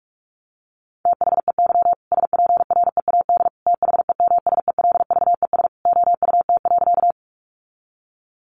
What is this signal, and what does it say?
Morse code, heard as a keyed tone, about 44 wpm, 700 Hz. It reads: THEY SPREAD THEMSELVES OUT.
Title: Sentences from Top 1200 Words 45wpm